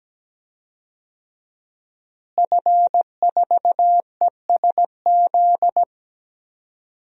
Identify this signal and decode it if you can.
Morse code, heard as a keyed tone, about 17 wpm, 700 Hz. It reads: F4ESZ
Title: DXCC-100 Call Signs - 5 Characters - Repeated 2x - 17wpm